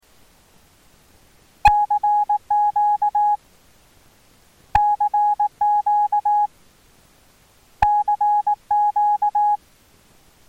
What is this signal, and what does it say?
Morse code, heard as a keyed tone, about 18 wpm, 830 Hz. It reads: CQ CQ CQ
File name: attachments / RX 28060 agc out res